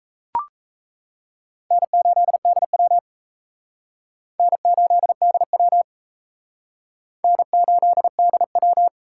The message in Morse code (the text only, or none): N8DW N8DW N8DW